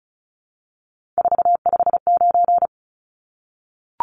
Morse code, heard as a keyed tone, about 34 wpm, 700 Hz. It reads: 459
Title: Numbers - 3-Digit 35wpm